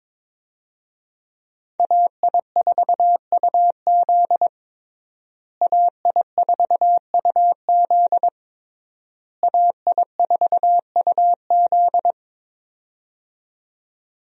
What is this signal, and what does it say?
Morse code, heard as a keyed tone, about 22 wpm, 700 Hz. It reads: AI4UZ AI4UZ AI4UZ
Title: USA Call Signs - 5 Characters - Repeated 3x - 22wpm